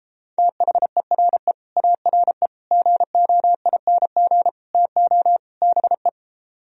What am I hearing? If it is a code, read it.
Morse code, heard as a keyed tone, about 33 wpm, 700 Hz. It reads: THERE ARE GOING TO BE